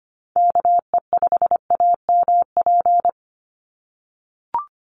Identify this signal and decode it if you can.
Morse code, heard as a keyed tone, about 25 wpm, 700 Hz. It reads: KE5AMP
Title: USA Call Signs 25wpm